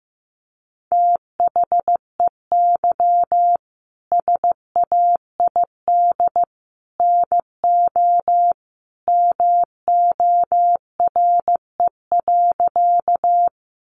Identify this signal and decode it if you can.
Morse code, heard as a keyed tone, about 15 wpm, 700 Hz. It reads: THEY SAID NO MORE.
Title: Sentences from Top 100 Words 15wpm